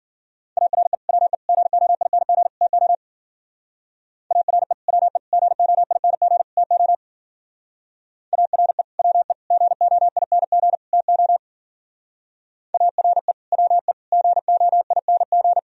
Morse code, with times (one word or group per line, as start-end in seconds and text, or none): 0.6-0.9 ARE
1.1-1.3 WE
1.5-2.5 GOING
2.6-2.9 TO
4.3-4.7 ARE
4.9-5.2 WE
5.3-6.4 GOING
6.6-6.9 TO
8.3-8.8 ARE
9.0-9.3 WE
9.5-10.8 GOING
10.9-11.4 TO
12.7-13.3 ARE
13.5-13.9 WE
14.1-15.6 GOING